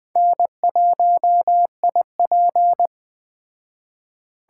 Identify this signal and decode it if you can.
Morse code, heard as a keyed tone, about 21 wpm, 700 Hz. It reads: N1IP